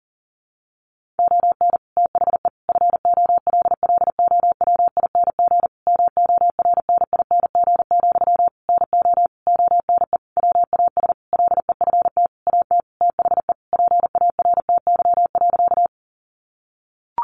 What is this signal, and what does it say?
Morse code, heard as a keyed tone, about 40 wpm, 700 Hz. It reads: ON THE FOLLOWING MORNING, NO ONE WAS LEFT AT THE PARTY.
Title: Sentences from Top 1400 Words 40wpm